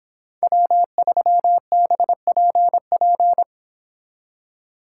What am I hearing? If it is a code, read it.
Morse code, heard as a keyed tone, about 25 wpm, 700 Hz. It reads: W3BPP
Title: Speed Racing - USA Call Signs 22wpm